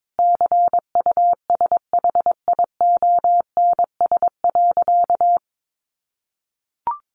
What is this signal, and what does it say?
Morse code, heard as a keyed tone, about 22 wpm, 700 Hz. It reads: CUSHIONS.
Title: A Sound Of Thunder 22wpm